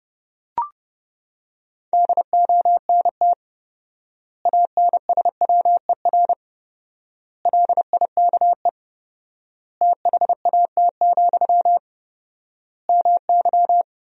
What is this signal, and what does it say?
Morse code, heard as a keyed tone, about 30 wpm, 700 Hz. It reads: DONT ANSWER LIKE THAT, MY
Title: 4x Word-Spacing - Sentences from Top 400 Words - 30wpm